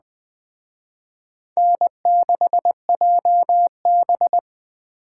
Morse code, heard as a keyed tone, about 20 wpm, 700 Hz. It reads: N6JB
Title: Speed Racing - USA 1x2 Call Signs 17wpm